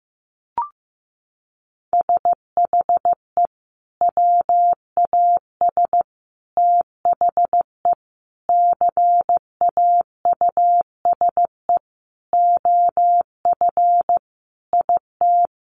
Morse code, heard as a keyed tone, about 15 wpm, 700 Hz. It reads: SHE WAS THE CAUSE OF IT
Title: Sentences from Top 200 Words 15wpm